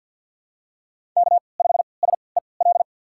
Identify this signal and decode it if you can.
Morse code, heard as a keyed tone, about 52 wpm, spaced as for 28 wpm, 700 Hz. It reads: K5SEL
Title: USA Call Signs - 3x Character Spacing - 50wpm